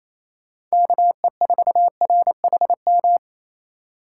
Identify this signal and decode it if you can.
Morse code, heard as a keyed tone, about 33 wpm, 700 Hz. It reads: KE4RHM
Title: Speed Racing - USA Call Signs 28wpm